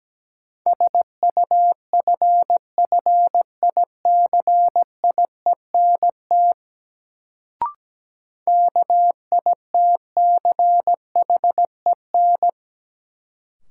Morse code, heard as a keyed tone, about 17 wpm, 700 Hz. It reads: SUFFICIENT KITCHEN
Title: Sentences from Top 1400 Words - New Word Drill - 17wpm